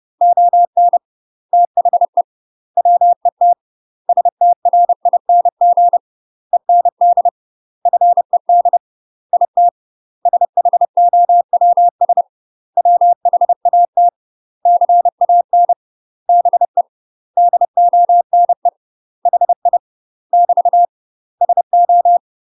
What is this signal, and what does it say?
Morse code, heard as a keyed tone, about 30 wpm, 700 Hz. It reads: ON THE WET STRING END FED IT SHOWS WHAT CAN BE DONE HI <BT> SO
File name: UK MASTER QSOs TRAINING / 30WPM / UK MASTER QSOs 0027